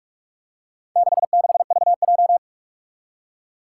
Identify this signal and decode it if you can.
Morse code, heard as a keyed tone, about 43 wpm, 700 Hz. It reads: 66VJ
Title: Four Challenging Letter-Number Combinations 45wpm